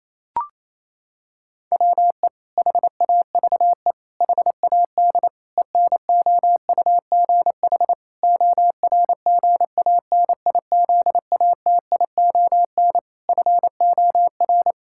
Morse code, heard as a keyed tone, about 28 wpm, 700 Hz. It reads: WE HAVE HAD ENOUGH ORGANIZATION FOR
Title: Sentences from Top 1000 Words - Review Part 5 - 28wpm